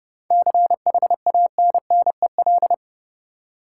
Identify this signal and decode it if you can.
Morse code, heard as a keyed tone, about 30 wpm, 700 Hz. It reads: CHANNEL
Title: Seven Letter Words - 30wpm